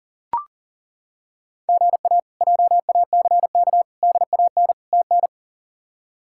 Morse code, heard as a keyed tone, about 40 wpm, 700 Hz. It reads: GA JACK DAN TN
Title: K1USN SST QSO Practice - 40wpm